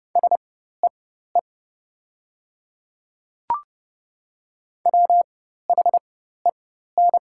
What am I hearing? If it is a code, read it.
Morse code, heard as a keyed tone, about 30 wpm, spaced as for 14 wpm, 690 Hz. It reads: SEE WHEN
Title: 4x Character Spacing - Top 200 Words - 30wpm